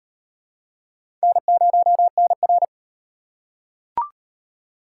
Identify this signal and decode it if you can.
Morse code, heard as a keyed tone, about 38 wpm, 700 Hz. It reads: N0NR